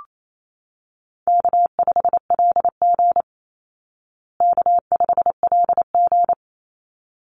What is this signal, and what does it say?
Morse code, heard as a keyed tone, about 28 wpm, 700 Hz. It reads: K5LG K5LG
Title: Easy Call Signs Repeated 2x - 28wpm